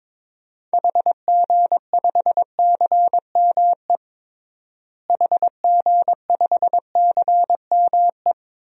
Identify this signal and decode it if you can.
Morse code, heard as a keyed tone, about 22 wpm, 700 Hz. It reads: HG5CME HG5CME
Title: DXCC-100 Call Signs - 6 Characters - Repeated 3x - 22wpm